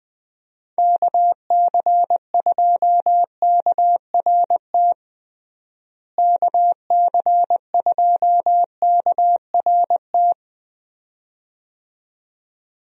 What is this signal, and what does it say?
Morse code, heard as a keyed tone, about 20 wpm, 700 Hz. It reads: KC2KRT KC2KRT
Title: USA Call Signs - 6 Characters - Repeated 2x - 20wpm